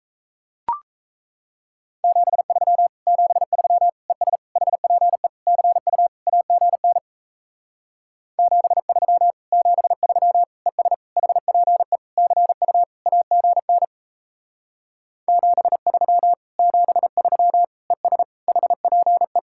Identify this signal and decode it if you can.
Morse code, heard as a keyed tone, about 37 wpm, 700 Hz. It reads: 73 73 ES HPE CU AGN 73 73 ES HPE CU AGN 73 73 ES HPE